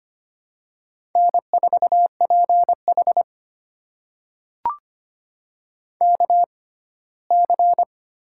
Morse code, heard as a keyed tone, about 25 wpm, spaced as for 17 wpm, 700 Hz. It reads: N4PH KC
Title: USA Call Signs - 6x Character Spacing - 25wpm